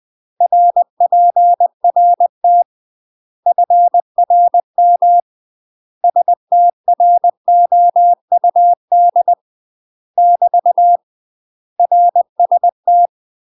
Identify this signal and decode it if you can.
Morse code, heard as a keyed tone, about 20 wpm, 700 Hz. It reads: RPRT FRM STROUD <BT> RST